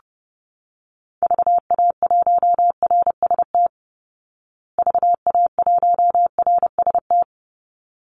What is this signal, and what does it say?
Morse code, heard as a keyed tone, about 30 wpm, 700 Hz. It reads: VA1RST VA1RST
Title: Call Signs Repeated 3x - 30wpm